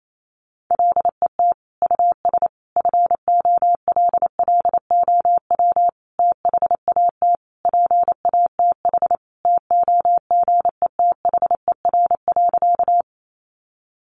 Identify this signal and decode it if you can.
Morse code, heard as a keyed tone, about 28 wpm, 700 Hz. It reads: LET US FOLLOW THAT PATH TOGETHER.
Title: Sentences from Top 1400 Words 28wpm